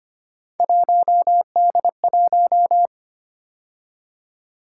Morse code, heard as a keyed tone, about 25 wpm, 700 Hz. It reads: 1D1
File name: Three Letter-Number Combinations - 25wpm